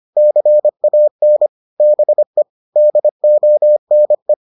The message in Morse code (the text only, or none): CAN BE DONE